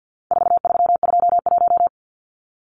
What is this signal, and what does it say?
Morse code, heard as a keyed tone, about 49 wpm, 700 Hz. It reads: 4321
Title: Numbers - 4-Digit 50wpm